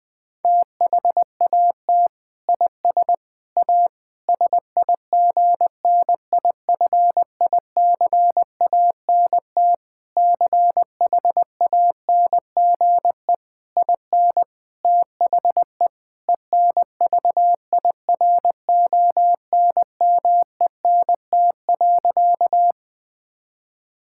Morse code, heard as a keyed tone, about 20 wpm, 700 Hz. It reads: THAT IS A SIGNIFICANT CHANGE IN THE ENVIRONMENT.
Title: Sentences from Top 1000 Words - Review Part 6 - 20wpm